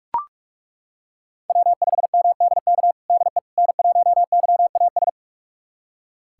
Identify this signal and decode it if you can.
Morse code, heard as a keyed tone, about 42 wpm, 700 Hz. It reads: W5MDK DE N1YAS
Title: USA Callsign DE Callsign - 5 Characters - 45wpm